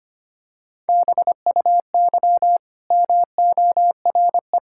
Morse code, heard as a keyed tone, about 25 wpm, 700 Hz. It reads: BUY MORE
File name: Easy Sentences with Hints - 25wpm